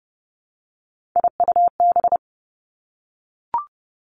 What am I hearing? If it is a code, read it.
Morse code, heard as a keyed tone, about 30 wpm, 700 Hz. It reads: IUB
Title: Three Challenging Letter-Number Combinations 30wpm